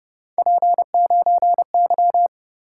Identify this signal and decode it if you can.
Morse code, heard as a keyed tone, about 29 wpm, 700 Hz. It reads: P9Y